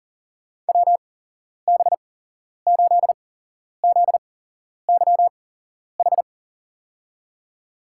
Morse code, heard as a keyed tone, about 40 wpm, spaced as for 11 wpm, 700 Hz. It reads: WB8ZYH
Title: USA Call Signs - 8x Character Spacing - 40wpm